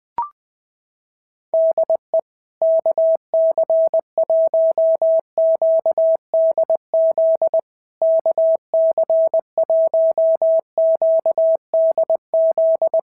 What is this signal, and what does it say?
Morse code, heard as a keyed tone, about 20 wpm, 650 Hz. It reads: DE KC1QDZ KC1QDZ
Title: POTA QSO Practice - 20wpm